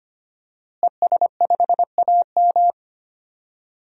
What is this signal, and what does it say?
Morse code, heard as a keyed tone, about 25 wpm, 700 Hz. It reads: ES5AM
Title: DXCC-100 Call Signs - 5 Characters - Repeated 3x - 25wpm